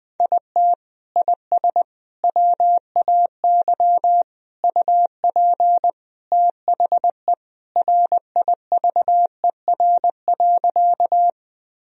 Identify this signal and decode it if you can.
Morse code, heard as a keyed tone, about 20 wpm, 700 Hz. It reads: IT IS WAY UP THE RIVER.
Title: Sentences from Top 1000 Words - Review Part 1 - 20wpm